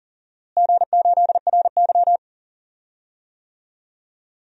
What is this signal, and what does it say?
Morse code, heard as a keyed tone, about 37 wpm, 700 Hz. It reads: G8RY
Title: DXCC-100 Call Signs - 4 Characters - Repeated 2x - 40wpm